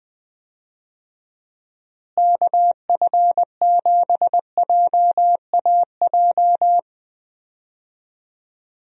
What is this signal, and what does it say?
Morse code, heard as a keyed tone, about 20 wpm, 700 Hz. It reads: KF7JAJ
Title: USA Call Signs - 6 Characters - Repeated 2x - 20wpm